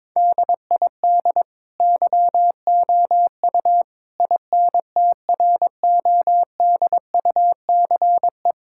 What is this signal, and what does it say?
Morse code, heard as a keyed tone, about 21 wpm, 700 Hz. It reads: DID YOU INTRODUCE
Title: Sentences from Top 1600 Words 22wpm